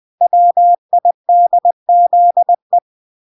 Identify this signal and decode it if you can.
Morse code, heard as a keyed tone, about 20 wpm, 700 Hz. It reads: WIDZE